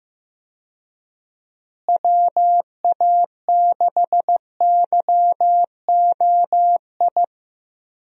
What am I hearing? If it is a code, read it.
Morse code, heard as a keyed tone, about 15 wpm, 700 Hz. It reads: WA6YOI
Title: USA Call Signs - 6 Characters - Repeated 3x - 15wpm